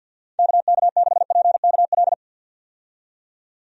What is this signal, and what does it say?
Morse code, heard as a keyed tone, about 48 wpm, 700 Hz. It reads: KK6PKL
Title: USA Call Signs - 6 Characters - 50wpm